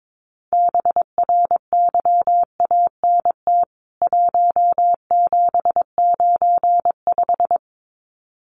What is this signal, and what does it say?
Morse code, heard as a keyed tone, about 22 wpm, 700 Hz. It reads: BRYANT 1795
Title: CWT Exchange 22wpm